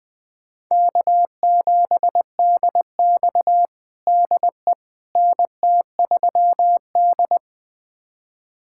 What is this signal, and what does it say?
Morse code, heard as a keyed tone, about 20 wpm, 700 Hz. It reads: K7DX DE NT3D
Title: USA Callsign DE Callsign - 4 Characters - 20wpm